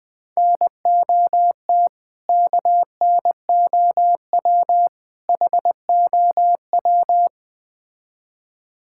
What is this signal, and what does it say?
Morse code, heard as a keyed tone, about 20 wpm, 700 Hz. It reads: NOT KNOW HOW
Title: Build A Phrase - 3-Word Phrases - 20wpm